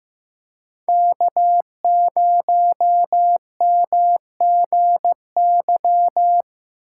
Morse code, heard as a keyed tone, about 15 wpm, 700 Hz. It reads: K0MGY
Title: USA Call Signs - 5 Characters - 6x Character Spacing - 15wpm